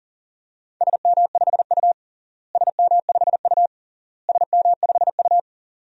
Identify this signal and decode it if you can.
Morse code, heard as a keyed tone, about 37 wpm, 700 Hz. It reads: SM5U SM5U SM5U
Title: DXCC-100 Call Signs - 4 Characters - Repeated 3x - 40wpm